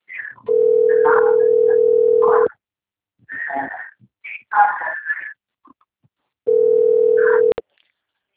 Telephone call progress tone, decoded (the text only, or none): ringback tone